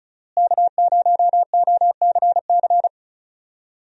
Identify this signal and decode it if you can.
Morse code, heard as a keyed tone, about 36 wpm, 700 Hz. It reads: K0OCC
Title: USA Call Signs - 5 Characters - Repeated 2x - 35wpm